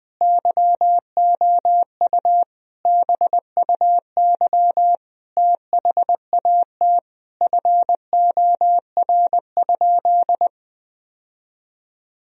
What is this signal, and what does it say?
Morse code, heard as a keyed tone, about 20 wpm, 700 Hz. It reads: YOU BUY THAT FOR?
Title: Sentences from Top 1000 Words - Review Part 4 - 20wpm